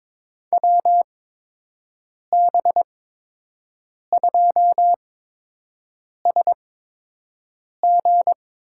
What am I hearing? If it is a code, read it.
Morse code, heard as a keyed tone, about 22 wpm, spaced as for 6 wpm, 700 Hz. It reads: WB2SG
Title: USA Call Signs - 6 Characters - 8x Character Spacing - 22wpm